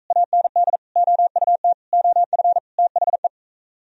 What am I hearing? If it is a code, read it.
Morse code, heard as a keyed tone, about 44 wpm, 700 Hz. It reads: AND OUT OF THE